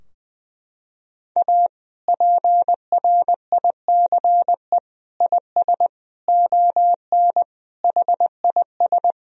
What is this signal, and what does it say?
Morse code, heard as a keyed tone, about 20 wpm, 700 Hz. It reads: A PRICE IS ON HIS
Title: Sentences from Top 1000 Words - Review Part 4 - 20wpm